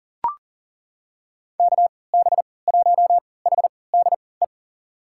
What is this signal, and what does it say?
Morse code, heard as a keyed tone, about 37 wpm, spaced as for 22 wpm, 700 Hz. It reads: KB1HDE
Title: USA Call Signs - 6 Characters - 3x Character Spacing - 40wpm